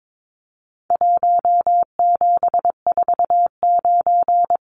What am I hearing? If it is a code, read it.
Morse code, heard as a keyed tone, about 22 wpm, 700 Hz. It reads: 1749
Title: Numbers - Contest Style - 4-Digit 22wpm